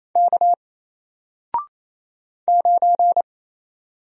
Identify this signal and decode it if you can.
Morse code, heard as a keyed tone, about 27 wpm, 700 Hz. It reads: K 9
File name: Single-Letter-Number-28wpm